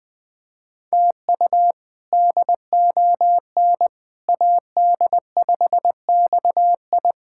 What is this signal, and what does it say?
Morse code, heard as a keyed tone, about 20 wpm, 700 Hz. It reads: TU DON AD5XI
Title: K1USN SST QSO Practice - 20wpm